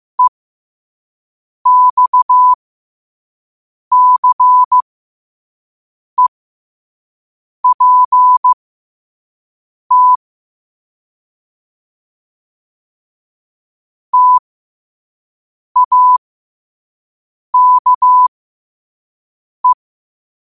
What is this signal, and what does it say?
Morse code, heard as a keyed tone, about 15 wpm, spaced as for 5 wpm, 1000 Hz. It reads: EXCEPT TAKE
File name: qotdfiles / 5wpm